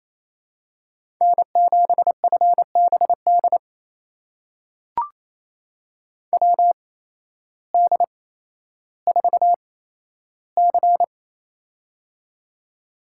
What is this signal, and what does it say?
Morse code, heard as a keyed tone, about 28 wpm, spaced as for 13 wpm, 700 Hz. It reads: N7FBD WD4C
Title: USA Call Signs - 8x Character Spacing - 28wpm